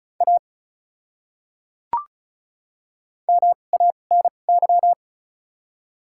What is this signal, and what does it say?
Morse code, heard as a keyed tone, about 36 wpm, spaced as for 25 wpm, 700 Hz. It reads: A MANY
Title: 2x Character Spacing - Top 100 Words - 35wpm